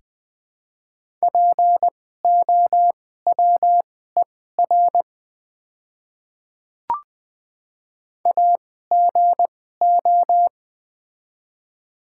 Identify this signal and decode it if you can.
Morse code, heard as a keyed tone, about 20 wpm, spaced as for 14 wpm, 700 Hz. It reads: POWER AGO
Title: 2x Character Spacing - Top 500 Words - 20wpm